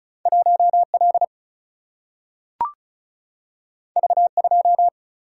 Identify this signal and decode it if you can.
Morse code, heard as a keyed tone, about 36 wpm, 700 Hz. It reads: 1 L V 2
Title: Single-Letter Single-Number 35wpm